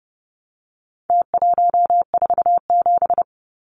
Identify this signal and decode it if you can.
Morse code, heard as a keyed tone, about 30 wpm, 700 Hz. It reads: T147
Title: Numbers - Contest Style - 4-Digit 30wpm